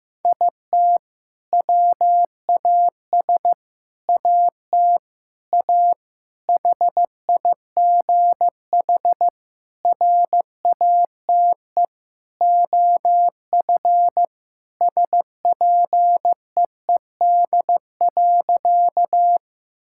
Morse code, heard as a keyed tone, about 15 wpm, 700 Hz. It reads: IT WAS AT A HIGH RATE OF SPEED.
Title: Sentences from Top 1000 Words - Review Part 5 - 15wpm